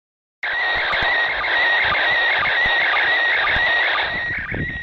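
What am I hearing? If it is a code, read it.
SSTV (slow-scan television), PD 120.
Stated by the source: SSTV image transmission from the ISS.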